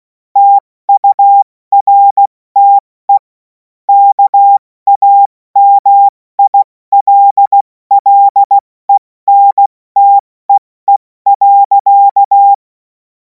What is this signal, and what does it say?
Morse code, heard as a keyed tone, about 16 wpm, 800 Hz. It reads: TURTE KAMILLENTEE.